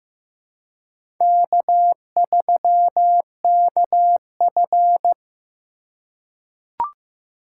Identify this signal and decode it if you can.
Morse code, heard as a keyed tone, about 15 wpm, 700 Hz. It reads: K3KF
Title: USA Call Signs - 2x Character Spacing - 15wpm